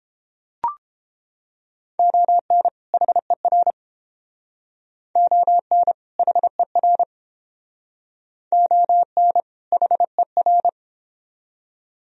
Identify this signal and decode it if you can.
Morse code, heard as a keyed tone, about 31 wpm, 700 Hz. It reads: ON HER ON HER ON HER